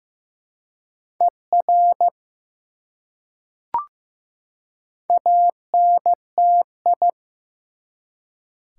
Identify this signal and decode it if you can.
Morse code, heard as a keyed tone, about 15 wpm, 700 Hz. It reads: ER ANTI
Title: Prefixes and Suffixes 15wpm